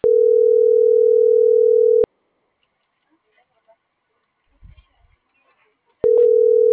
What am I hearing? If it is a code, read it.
Ringback tone (North America), 440 + 480 Hz.